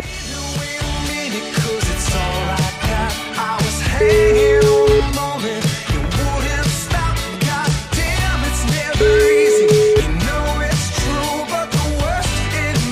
Ringback tone (Europe, 425 Hz), with noise.